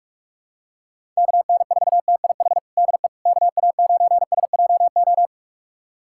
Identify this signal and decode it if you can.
Morse code, heard as a keyed tone, about 45 wpm, 700 Hz. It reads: KN4TIH DE KA9SJO
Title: USA Callsign DE Callsign - 6 Characters - 45wpm